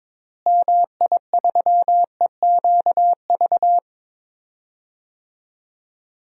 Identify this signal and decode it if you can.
Morse code, heard as a keyed tone, about 22 wpm, 700 Hz. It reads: MI3EQV
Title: DXCC-100 Call Signs - 6 Characters - Repeated 3x - 22wpm